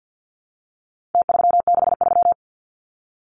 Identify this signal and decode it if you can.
Morse code, heard as a keyed tone, about 50 wpm, 700 Hz. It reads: T363
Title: Numbers - Contest Style - 4-Digit 50wpm